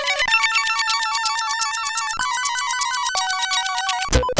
There is no code.